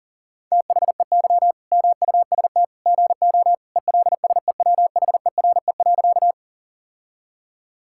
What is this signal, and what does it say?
Morse code, heard as a keyed tone, about 37 wpm, 700 Hz. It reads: THEY MUST GO ELSEWHERE.
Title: Sentences from Top 1600 Words 40wpm